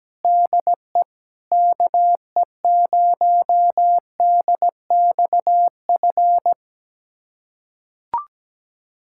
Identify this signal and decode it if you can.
Morse code, heard as a keyed tone, about 17 wpm, 700 Hz. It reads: DE KE0DXF
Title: USA Callsign DE Callsign - 6 Characters - 17wpm